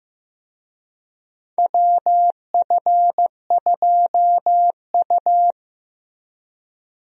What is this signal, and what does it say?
Morse code, heard as a keyed tone, about 15 wpm, 700 Hz. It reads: WF2U
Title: USA Call Signs - 4 Characters - Repeated 3x - 15wpm